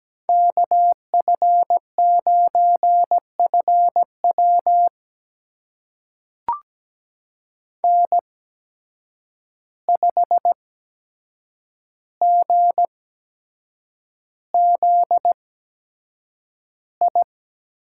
Morse code, heard as a keyed tone, about 17 wpm, spaced as for 7 wpm, 700 Hz. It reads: KF9FW N5GZI
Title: USA Call Signs - 5 Characters - 8x Character Spacing - 17wpm